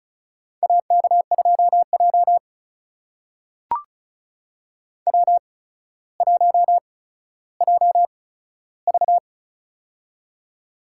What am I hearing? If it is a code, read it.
Morse code, heard as a keyed tone, about 36 wpm, spaced as for 15 wpm, 700 Hz. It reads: AK2J W1JV
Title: USA Call Signs - 4 Characters - 8x Character Spacing - 35wpm